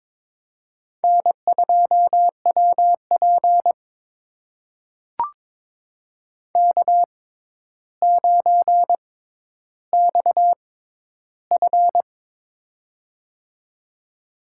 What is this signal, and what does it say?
Morse code, heard as a keyed tone, about 21 wpm, spaced as for 11 wpm, 700 Hz. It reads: N2WP K9XF
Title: USA Call Signs - 4 Characters - 6x Character Spacing - 22wpm